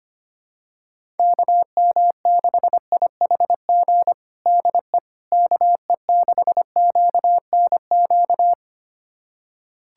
Morse code, heard as a keyed tone, about 25 wpm, 700 Hz. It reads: KM6IHG DE KE6QNQ
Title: USA Callsign DE Callsign - 6 Characters - 25wpm